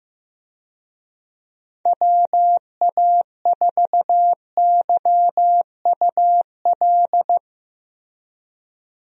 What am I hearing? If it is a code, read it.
Morse code, heard as a keyed tone, about 15 wpm, 700 Hz. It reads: WA4YUL
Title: USA Call Signs - 6 Characters - Repeated 3x - 15wpm